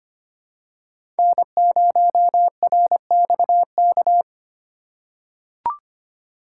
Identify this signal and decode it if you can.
Morse code, heard as a keyed tone, about 25 wpm, 700 Hz. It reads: N0RXK
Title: USA Call Signs - 5 Characters - 2x Character Spacing - 25wpm